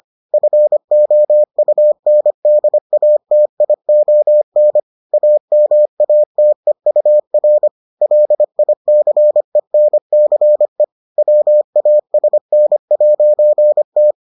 Morse code, heard as a keyed tone, about 25 wpm, 600 Hz. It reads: FOUNDATION AMATEUR LICENCE WASN'T